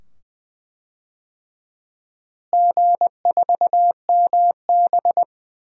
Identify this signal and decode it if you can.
Morse code, heard as a keyed tone, about 20 wpm, 700 Hz. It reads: G4MB
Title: DXCC-100 Call Signs - 4 Characters - Repeated 2x - 20wpm